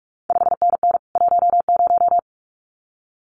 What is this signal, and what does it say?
Morse code, heard as a keyed tone, about 46 wpm, 700 Hz. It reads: 5NN 10
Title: CQ WW DX Contest Exchange Encore - 45wpm